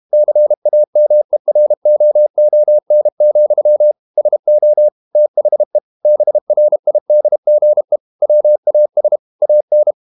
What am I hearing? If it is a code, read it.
Morse code, heard as a keyed tone, about 32 wpm, 600 Hz. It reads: CAMEROON, SO THE BRIDGE WAS AN
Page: Never 31wpm | CW med Gnister